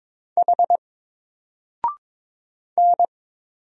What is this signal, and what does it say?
Morse code, heard as a keyed tone, about 22 wpm, 700 Hz. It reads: H N
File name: Single-Letter-Number-22wpm